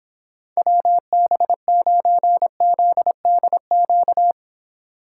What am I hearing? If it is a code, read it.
Morse code, heard as a keyed tone, about 26 wpm, 700 Hz. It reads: WB9ZDQ